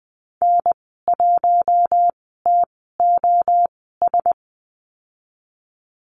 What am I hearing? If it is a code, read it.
Morse code, heard as a keyed tone, about 20 wpm, spaced as for 15 wpm, 700 Hz. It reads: N1TOS
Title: 042 Lesson 8g - Number 1 - Call Signs 20wpm